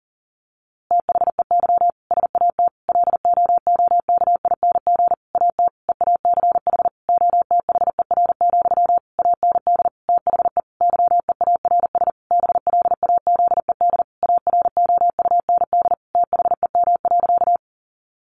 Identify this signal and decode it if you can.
Morse code, heard as a keyed tone, about 40 wpm, 700 Hz. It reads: THEY SAT LOOKING AT EACH OTHER, AND THE YEARS BLAZED AROUND THEM.
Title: A Sound Of Thunder 40wpm